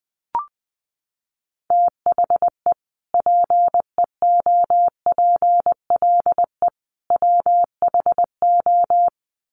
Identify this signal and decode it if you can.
Morse code, heard as a keyed tone, about 19 wpm, 700 Hz. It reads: THE PEOPLE WHO
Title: Sets of 3 Words 20wpm